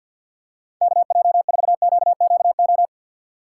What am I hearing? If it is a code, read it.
Morse code, heard as a keyed tone, about 48 wpm, 700 Hz. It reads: KJ4QQO
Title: USA Call Signs - 6 Characters - 3x Character Spacing - 50wpm